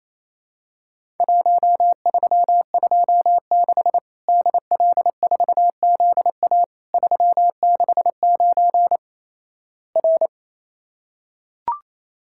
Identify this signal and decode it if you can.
Morse code, heard as a keyed tone, about 27 wpm, 700 Hz. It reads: 1326 DL4ZA 369 R
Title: Worked All Europe DX Contest - QTC Exchange - 28wpm